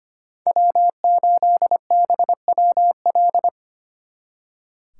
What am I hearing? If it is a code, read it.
Morse code, heard as a keyed tone, about 30 wpm, 700 Hz. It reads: W8BWL
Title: Speed Racing - USA 1x3 Call Signs 25wpm